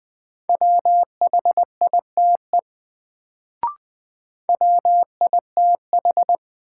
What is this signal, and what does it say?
Morse code, heard as a keyed tone, about 20 wpm, 700 Hz. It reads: WHITE WITH
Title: Sets of 2 Words - New Word Drill - 20wpm